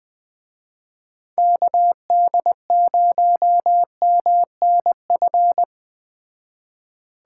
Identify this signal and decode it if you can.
Morse code, heard as a keyed tone, about 20 wpm, 700 Hz. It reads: KD0MNF
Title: Speed Racing - USA Call Signs 17wpm